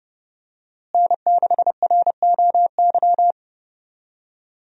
Morse code, heard as a keyed tone, about 30 wpm, 700 Hz. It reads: N6ROY
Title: USA Call Signs - 5 Characters - Repeated 3x - 30wpm